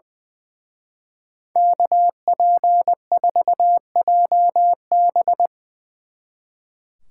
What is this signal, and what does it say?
Morse code, heard as a keyed tone, about 22 wpm, 700 Hz. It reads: KP4JB
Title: Speed Racing - USA 2x2 Call Signs 20wpm